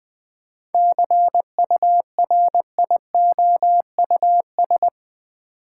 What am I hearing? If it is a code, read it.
Morse code, heard as a keyed tone, about 21 wpm, 700 Hz. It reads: CURIOUS
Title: 7 Letter Words - First 3 Letters Given - 20wpm